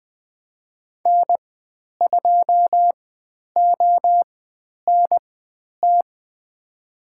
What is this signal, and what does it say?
Morse code, heard as a keyed tone, about 20 wpm, spaced as for 10 wpm, 700 Hz. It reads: N2ONT
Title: 095 Lesson 16g - Letter c - Call Signs 20wpm